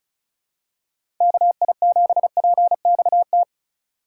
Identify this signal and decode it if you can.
Morse code, heard as a keyed tone, about 34 wpm, 700 Hz. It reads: KI7PXT
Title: USA Call Signs - 6 Characters - 3x Character Spacing - 35wpm